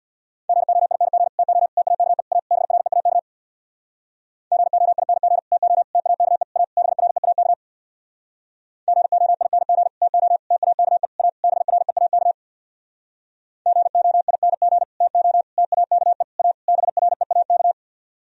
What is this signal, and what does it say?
Morse code, heard as a keyed tone, about 64 wpm, 700 Hz. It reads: GOING TO TAKE A BREAK GOING TO TAKE A BREAK GOING TO TAKE A BREAK GOING TO TAKE A BREAK